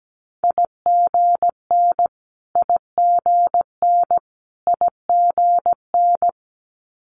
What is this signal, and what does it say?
Morse code, heard as a keyed tone, about 17 wpm, 700 Hz. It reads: IGN IGN IGN
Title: Multi-letter Phonemes Repeated 3x 17wpm